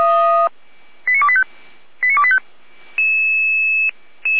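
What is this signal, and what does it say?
Five-tone selective call (ZVEI), heard as quick runs of five tones, tones of about 70 ms. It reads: 89285 89285
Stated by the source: ZVEI five-tone selective call 89020, received by SDR.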